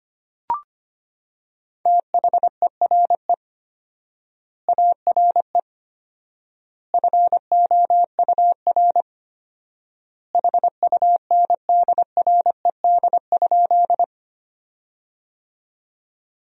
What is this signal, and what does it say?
Morse code, heard as a keyed tone, about 25 wpm, 700 Hz. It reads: THERE ARE FOUR HUNDRED?
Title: 4x Word-Spacing - Sentences from Top 400 Words - 25wpm